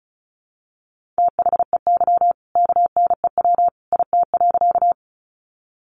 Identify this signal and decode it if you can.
Morse code, heard as a keyed tone, about 34 wpm, 700 Hz. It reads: THEY KNEW IT.
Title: Sentences from Top 400 Words 35wpm